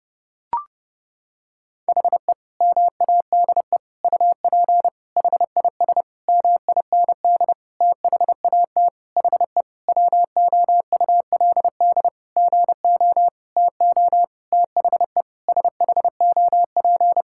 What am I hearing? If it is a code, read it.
Morse code, heard as a keyed tone, about 32 wpm, 700 Hz. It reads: HE MADE UP HIS MIND THAT HE WOULD GO TO THE SHOP
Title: Sentences from Top 1600 Words 30wpm